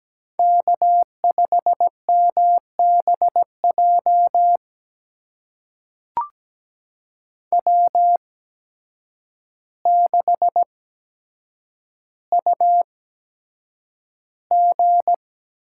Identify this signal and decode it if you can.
Morse code, heard as a keyed tone, about 17 wpm, spaced as for 8 wpm, 700 Hz. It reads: K5MBJ W6UG
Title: USA Call Signs - 8x Character Spacing - 17wpm